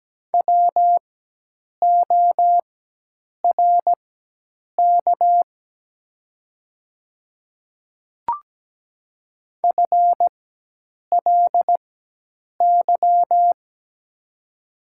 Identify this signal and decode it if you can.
Morse code, heard as a keyed tone, about 17 wpm, spaced as for 8 wpm, 700 Hz. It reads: WORK FLY
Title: 4x Character Spacing - Top 500 Words - 17wpm